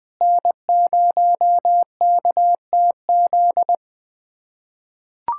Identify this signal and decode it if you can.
Morse code, heard as a keyed tone, about 20 wpm, 700 Hz. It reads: N0KTZ
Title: USA Callsign DE Callsign - 5 Characters - 20wpm